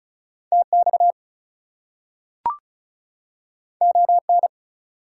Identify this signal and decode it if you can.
Morse code, heard as a keyed tone, about 36 wpm, 700 Hz. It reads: TX ON
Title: US States and Canadian Provinces 35wpm